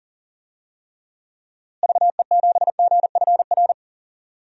Morse code, heard as a keyed tone, about 37 wpm, 700 Hz. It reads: VE7GFR
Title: DXCC-100 Call Signs - 6 Characters - Repeated 3x - 40wpm